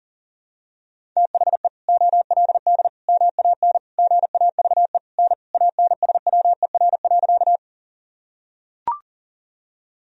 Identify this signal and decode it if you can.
Morse code, heard as a keyed tone, about 40 wpm, 700 Hz. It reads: THE OLD MAN GAVE N ANSWER.
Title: Sentences from Top 500 Words and Abbreviations 40wpm